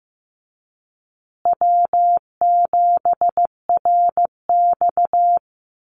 Morse code, heard as a keyed tone, about 15 wpm, 700 Hz. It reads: W7RX
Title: Call Signs Repeated 2x - 15wpm